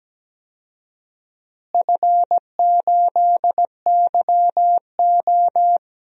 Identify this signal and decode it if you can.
Morse code, heard as a keyed tone, about 17 wpm, 700 Hz. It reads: F8YO
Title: DXCC-100 Call Signs - 4 Characters - Repeated 2x - 17wpm